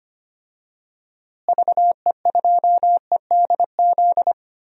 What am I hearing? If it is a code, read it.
Morse code, heard as a keyed tone, about 25 wpm, 690 Hz. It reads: VE2EDZ
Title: DXCC-100 Call Signs - 6 Characters - Repeated 3x - 25wpm